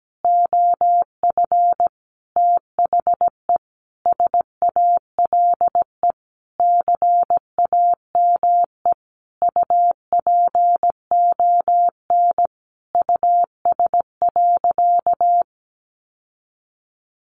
Morse code, heard as a keyed tone, about 17 wpm, 700 Hz. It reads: OF THE SALE CAME UPON US.
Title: Sentences from Top 1400 Words 17wpm